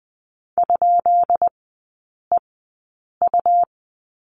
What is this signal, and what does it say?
Morse code, heard as a keyed tone, about 20 wpm, spaced as for 8 wpm, 700 Hz. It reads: ?EU
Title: 133 Lesson 22e - Question Mark - Three Characters 20wpm